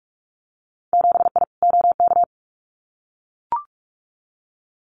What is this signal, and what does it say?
Morse code, heard as a keyed tone, about 44 wpm, 700 Hz. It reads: 7I OK
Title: Winter Field Day Exchanges Encore 45wpm